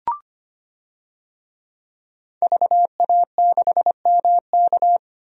Morse code, heard as a keyed tone, about 25 wpm, 700 Hz. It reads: VA6MK
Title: DXCC-100 Call Signs - 5 Characters - 25wpm